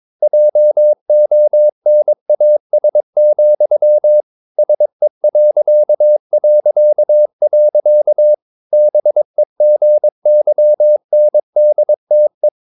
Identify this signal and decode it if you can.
Morse code, heard as a keyed tone, about 22 wpm, 600 Hz. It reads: JONAS, SE... BEGYNDTE
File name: Ferie QRS 22wpm